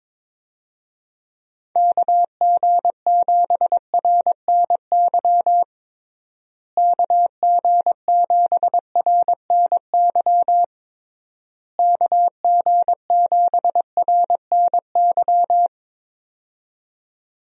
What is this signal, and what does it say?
Morse code, heard as a keyed tone, about 21 wpm, 700 Hz. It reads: KG7RNY KG7RNY KG7RNY
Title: USA Call Signs - 6 Characters - Repeated 3x - 22wpm